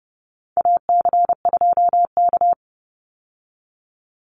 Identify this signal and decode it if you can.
Morse code, heard as a keyed tone, about 30 wpm, 700 Hz. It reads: AC2K
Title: Call Signs - 30wpm